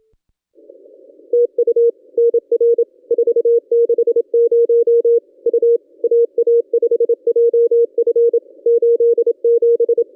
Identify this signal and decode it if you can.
Morse code, heard as a keyed tone, about 27 wpm, 470 Hz. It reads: TU NR 460 U AA5JF 87
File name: contests / ss-cw-k5tr-op-n5rz -1200